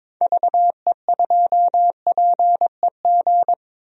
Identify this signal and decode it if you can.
Morse code, heard as a keyed tone, about 22 wpm, 700 Hz. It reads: VE2PEG
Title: DXCC-100 Call Signs - 6 Characters - 22wpm